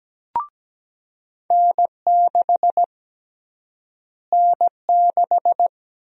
Morse code, heard as a keyed tone, about 17 wpm, 700 Hz. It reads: N6 N6
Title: USA Call Sign Prefixes - Repeated 2x - 17wpm